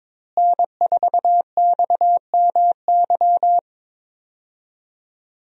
Morse code, heard as a keyed tone, about 22 wpm, 700 Hz. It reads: N4XMY
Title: USA Call Signs - 5 Characters - Repeated 3x - 22wpm